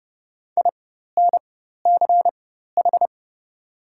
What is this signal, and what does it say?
Morse code, heard as a keyed tone, about 30 wpm, spaced as for 14 wpm, 700 Hz. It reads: INCH
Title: 4x Character Spacing - Top 500 Words - 30wpm